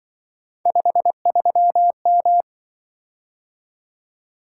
Morse code, heard as a keyed tone, about 25 wpm, 700 Hz. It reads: 53M
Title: Speed-Racing - Three Letter-Number Combinations - 20wpm